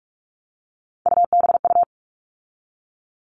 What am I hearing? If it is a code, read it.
Morse code, heard as a keyed tone, about 44 wpm, 700 Hz. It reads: UBU
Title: All Cut Numbers - 3-Digit 45wpm